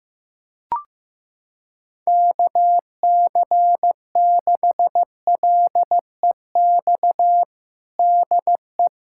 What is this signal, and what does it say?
Morse code, heard as a keyed tone, about 15 wpm, 700 Hz. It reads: KC6LEX DE
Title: USA Callsign DE Callsign 15wpm